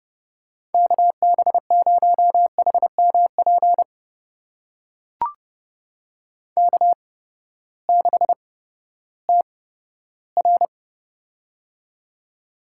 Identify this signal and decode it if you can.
Morse code, heard as a keyed tone, about 30 wpm, spaced as for 15 wpm, 700 Hz. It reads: KB0HMP K6TR
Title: USA Call Signs - 8x Character Spacing - 30wpm